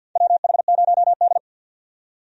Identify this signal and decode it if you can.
Morse code, heard as a keyed tone, about 53 wpm, 700 Hz. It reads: WH0D
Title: USA Call Signs - 4 Characters - 50wpm